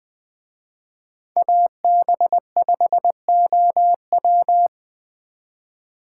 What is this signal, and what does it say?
Morse code, heard as a keyed tone, about 21 wpm, 700 Hz. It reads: AB5OW